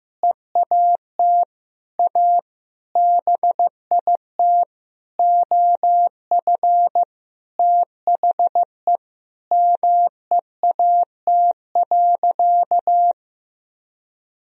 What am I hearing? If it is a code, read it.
Morse code, heard as a keyed tone, about 15 wpm, 700 Hz. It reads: EAT A BIT OF THE MEAT.
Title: Sentences from Top 1000 Words - Review Part 3 - 15wpm